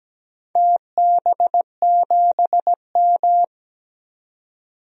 Morse code, heard as a keyed tone, about 17 wpm, 700 Hz. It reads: TB7M
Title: DXCC-100 Call Signs - 4 Characters - 17wpm